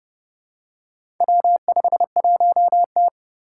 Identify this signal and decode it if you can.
Morse code, heard as a keyed tone, about 30 wpm, 700 Hz. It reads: W51T
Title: Four Challenging Letter-Number Combinations 30wpm